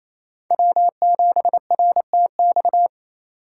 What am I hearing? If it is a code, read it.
Morse code, heard as a keyed tone, about 28 wpm, 700 Hz. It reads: W7RTX
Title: USA Call Signs - 3x Character Spacing - 28wpm